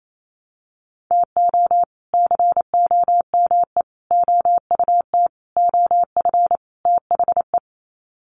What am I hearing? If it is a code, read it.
Morse code, heard as a keyed tone, about 28 wpm, 700 Hz. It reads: TO COME OUT OF THE
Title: Sets of 5 Words Encore2 28wpm